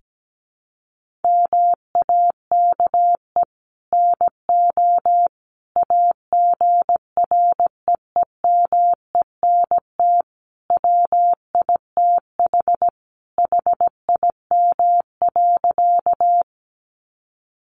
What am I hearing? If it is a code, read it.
Morse code, heard as a keyed tone, about 17 wpm, 700 Hz. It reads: MAKE NO AGREEMENT WITH HIM.
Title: Sentences from Top 1500 Words 17wpm